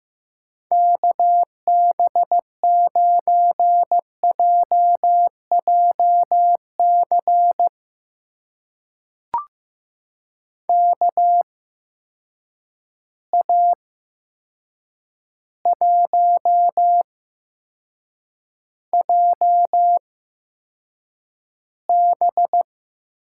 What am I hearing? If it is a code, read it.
Morse code, heard as a keyed tone, about 15 wpm, spaced as for 7 wpm, 700 Hz. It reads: KB9JJC KA1JB
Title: USA Call Signs - 6 Characters - 8x Character Spacing - 15wpm